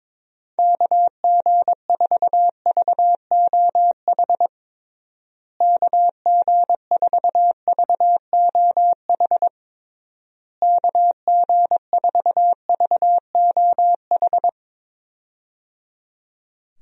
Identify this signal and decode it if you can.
Morse code, heard as a keyed tone, about 21 wpm, 700 Hz. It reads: KG4VOH KG4VOH KG4VOH
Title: USA Call Signs - 6 Characters - Repeated 3x - 22wpm